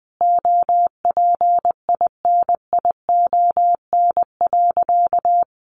Morse code, heard as a keyed tone, about 20 wpm, 700 Hz. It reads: OPINION.
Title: Sentences from Top 1200 Words 20wpm